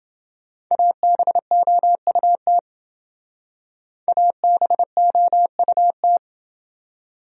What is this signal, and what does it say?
Morse code, heard as a keyed tone, about 28 wpm, 700 Hz. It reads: ABOUT ABOUT